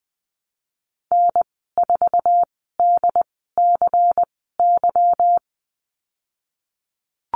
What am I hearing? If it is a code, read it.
Morse code, heard as a keyed tone, about 20 wpm, spaced as for 14 wpm, 700 Hz. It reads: N4DCY
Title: 147 Lesson 24g - Letter y - Call Signs 20wpm